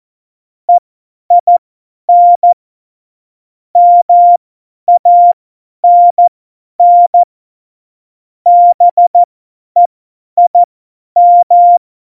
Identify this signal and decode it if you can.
Morse code, heard as a keyed tone, about 14 wpm, spaced as for 10 wpm, 700 Hz. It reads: EIN MANN BEIM
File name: Morsekurs / Witze / Witz 23. Januar 2018, 50 F70